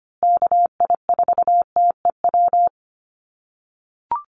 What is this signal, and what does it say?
Morse code, heard as a keyed tone, about 25 wpm, 700 Hz. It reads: KI4TEW
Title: USA Call Signs Repeated 3x 25wpm